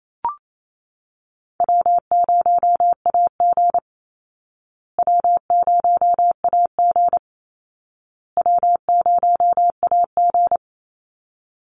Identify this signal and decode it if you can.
Morse code, heard as a keyed tone, about 28 wpm, 700 Hz. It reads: W0AG W0AG W0AG
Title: Call Signs Repeated 2x - 28wpm